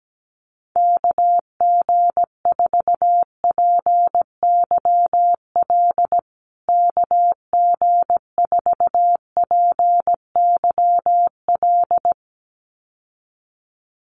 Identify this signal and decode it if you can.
Morse code, heard as a keyed tone, about 17 wpm, 700 Hz. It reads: KG4PYL KG4PYL
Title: USA Call Signs Repeated 2x 17wpm